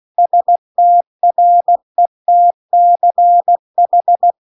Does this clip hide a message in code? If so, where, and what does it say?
Yes: Morse code from 0.2 to 4.3 s, heard as a keyed tone, about 16 wpm, 700 Hz. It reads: STRETCH